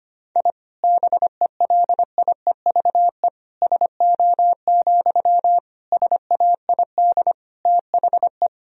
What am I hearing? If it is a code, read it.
Morse code, heard as a keyed tone, about 25 wpm, 700 Hz. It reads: I BELIEVE SO, SAID THE
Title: Sentences from Top 1000 Words - Review Part 4 - 25wpm